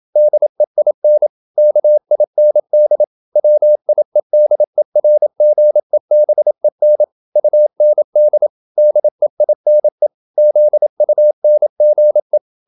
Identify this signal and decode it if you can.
Morse code, heard as a keyed tone, about 27 wpm, 600 Hz. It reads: DEIN KIND WIEDERGEBEN UND DEINE ZUNGE